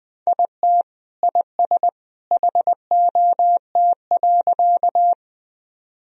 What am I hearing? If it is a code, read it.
Morse code, heard as a keyed tone, about 21 wpm, 700 Hz. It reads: IT IS HOT.
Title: Sentences from Top 1000 Words - Review Part 3 - 20wpm